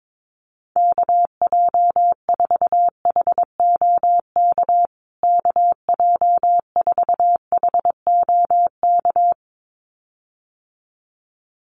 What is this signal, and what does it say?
Morse code, heard as a keyed tone, about 22 wpm, 700 Hz. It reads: KJ4HOK KJ4HOK
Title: USA Call Signs Repeated 2x 22wpm